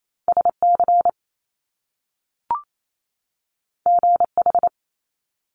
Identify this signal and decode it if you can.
Morse code, heard as a keyed tone, about 28 wpm, 700 Hz. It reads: SC GH
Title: Top Two Letter Combinations in Words 28wpm